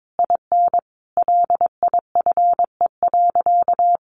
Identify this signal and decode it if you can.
Morse code, heard as a keyed tone, about 22 wpm, 700 Hz. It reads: IN LIFE.
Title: Sentences from Top 1200 Words 22wpm